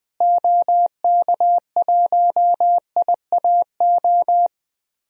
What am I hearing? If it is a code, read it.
Morse code, heard as a keyed tone, about 20 wpm, 700 Hz. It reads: OK1IAO
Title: DXCC-100 Call Signs - 6 Characters - Repeated 3x - 20wpm